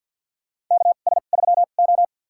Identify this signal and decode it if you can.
Morse code, heard as a keyed tone, about 47 wpm, spaced as for 35 wpm, 700 Hz. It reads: KS3O
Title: USA Call Signs - 2x Character Spacing - 50wpm